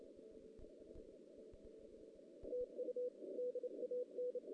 Morse code, heard as a keyed tone, about 29 wpm, 500 Hz. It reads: TU DAN
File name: 2016 JAN CW NAQP / 2016 Jan naqp k5ot - 0-11